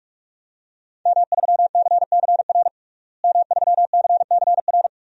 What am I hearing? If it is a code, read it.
Morse code, heard as a keyed tone, about 42 wpm, 700 Hz. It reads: M3CCR M3CCR
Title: DXCC-100 Call Signs - 5 Characters - Repeated 2x - 45wpm